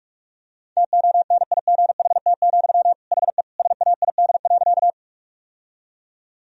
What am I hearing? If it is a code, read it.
Morse code, heard as a keyed tone, about 41 wpm, 700 Hz. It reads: TONIGHT, HE SAID.
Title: Sentences from Top 1000 Words - Review Part 3 - 45wpm